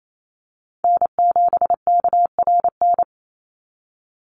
Morse code, heard as a keyed tone, about 28 wpm, 700 Hz. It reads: N7KRN
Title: NA SOTA Chaser Callsigns 28wpm